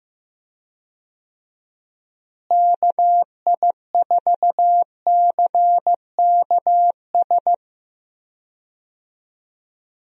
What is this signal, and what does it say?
Morse code, heard as a keyed tone, about 15 wpm, 700 Hz. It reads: KI4CKS
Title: USA Call Signs - 6 Characters - Repeated 3x - 15wpm